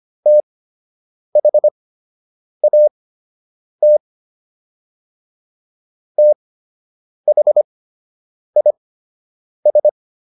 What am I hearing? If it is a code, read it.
Morse code, heard as a keyed tone, about 25 wpm, spaced as for 8 wpm, 600 Hz. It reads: THAT THIS